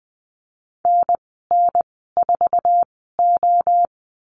Morse code, heard as a keyed tone, about 20 wpm, spaced as for 14 wpm, 700 Hz. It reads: NN4O
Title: 048 Lesson 9g - Number 4 - Call Signs 20wpm